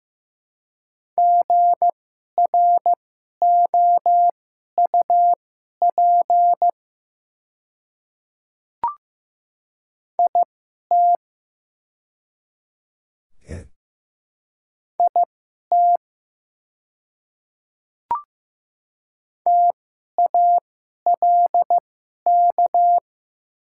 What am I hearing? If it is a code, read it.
Morse code, heard as a keyed tone, about 15 wpm, spaced as for 11 wpm, 700 Hz. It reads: GROUP IT IT TALK
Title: 2x Character Spacing - Top 500 Words - 15wpm